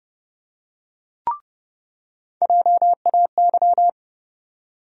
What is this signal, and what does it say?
Morse code, heard as a keyed tone, about 30 wpm, 700 Hz. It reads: JAY
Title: USA Call Sign Postfixes - 3 Letters 30wpm